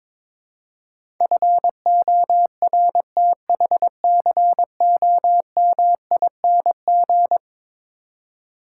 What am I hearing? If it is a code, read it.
Morse code, heard as a keyed tone, about 22 wpm, 700 Hz. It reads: FORTHCOMING
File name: practice / 22wpm / Eleven Letter Words - 22wpm